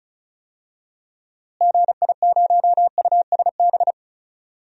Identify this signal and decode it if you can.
Morse code, heard as a keyed tone, about 34 wpm, 700 Hz. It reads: GI0USB
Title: DXCC-100 Call Signs - 6 Characters - Repeated 3x - 35wpm